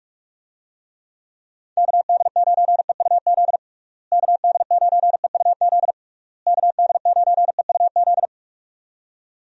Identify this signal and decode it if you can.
Morse code, heard as a keyed tone, about 41 wpm, 700 Hz. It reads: KD9EUZ KD9EUZ KD9EUZ
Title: USA Call Signs - 6 Characters - Repeated 3x - 45wpm